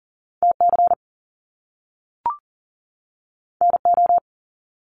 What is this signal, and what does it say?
Morse code, heard as a keyed tone, about 40 wpm, 700 Hz. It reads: TC NO
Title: Top Two Letter Combinations in Words 40wpm